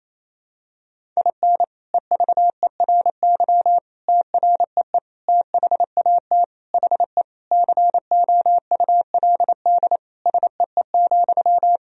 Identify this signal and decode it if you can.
Morse code, heard as a keyed tone, about 28 wpm, 700 Hz. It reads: IN EVERY TREE THAT HE COULD SEE,
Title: Sentences from Top 1000 Words - Review Part 1 - 28wpm